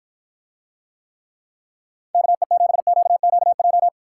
Morse code, heard as a keyed tone, about 54 wpm, 700 Hz. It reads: KE7QQJ